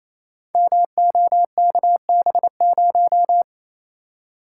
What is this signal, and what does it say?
Morse code, heard as a keyed tone, about 28 wpm, 700 Hz. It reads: MOKB0
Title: Code Groups - 5 Letter:Numbers - 28wpm